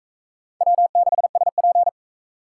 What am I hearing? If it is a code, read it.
Morse code, heard as a keyed tone, about 40 wpm, 700 Hz. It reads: W6SP